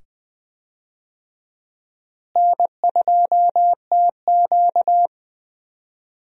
Morse code, heard as a keyed tone, about 20 wpm, 700 Hz. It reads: N2TQ
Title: USA Call Signs - 4 Characters - Repeated 2x - 20wpm